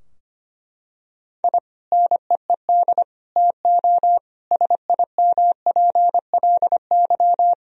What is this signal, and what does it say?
Morse code, heard as a keyed tone, about 25 wpm, 700 Hz. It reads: I NEED TO SIMPLY
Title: Sentences from Top 1000 Words - Review Part 4 - 25wpm